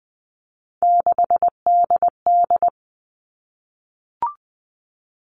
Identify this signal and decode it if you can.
Morse code, heard as a keyed tone, about 20 wpm, 700 Hz. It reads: 6DD
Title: All Cut Numbers - 3-Digit 20wpm